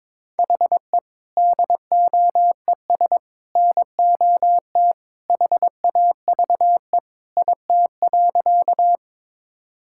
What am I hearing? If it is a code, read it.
Morse code, heard as a keyed tone, about 22 wpm, 700 Hz. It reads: HE DOES NOT HAVE IT.
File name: practice / 22wpm / Sentences with 3-Word Phrases - 22wpm